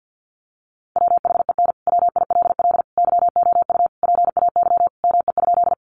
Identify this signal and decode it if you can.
Morse code, heard as a keyed tone, about 48 wpm, 700 Hz. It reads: WHEN WILL YOU PAY ME?
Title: Sentences from Top 700 Words 50wpm